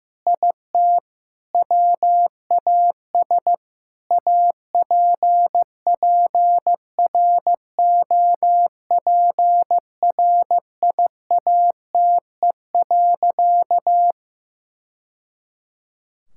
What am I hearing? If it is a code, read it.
Morse code, heard as a keyed tone, about 15 wpm, 700 Hz. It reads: IT WAS APPROPRIATE.
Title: Sentences from Top 1000 Words - Review Part 5 - 15wpm